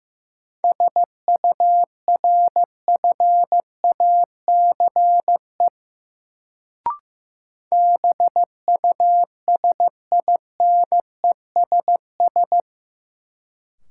Morse code, heard as a keyed tone, about 15 wpm, 700 Hz. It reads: SURFACE BUSINESS
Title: Sentences from Top 600 Words - New Word Drill - 15wpm